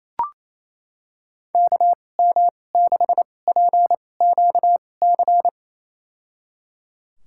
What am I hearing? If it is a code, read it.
Morse code, heard as a keyed tone, about 28 wpm, spaced as for 20 wpm, 700 Hz. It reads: KM6PQC
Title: USA Call Signs - 6 Characters - 2x Character Spacing - 28wpm